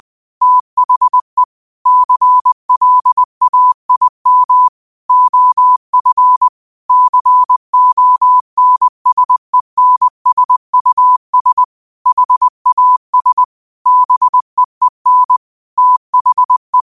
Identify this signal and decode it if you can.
Morse code, heard as a keyed tone, about 20 wpm, 1000 Hz. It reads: THE CLAIM OF CONSENSUS HAS BEEN THE
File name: qotdfiles / 20wpm